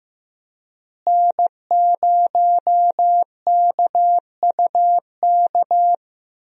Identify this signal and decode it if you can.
Morse code, heard as a keyed tone, about 15 wpm, 700 Hz. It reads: N0KUK
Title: USA Call Signs - 5 Characters - Repeated 2x - 15wpm